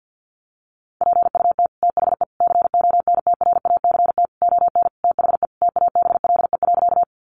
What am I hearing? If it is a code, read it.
Morse code, heard as a keyed tone, about 50 wpm, 700 Hz. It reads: PUT THE CONTRACT ON THE TABLE.
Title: Sentences from Top 1400 Words 50wpm